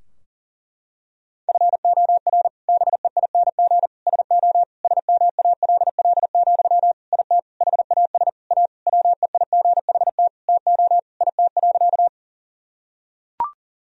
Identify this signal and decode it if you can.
Morse code, heard as a keyed tone, about 40 wpm, 700 Hz. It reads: FOR BEING SO SMALL, IT HAS A WEIGHT TO IT.
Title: Sentences from Top 1000 Words - Review Part 3 - 40wpm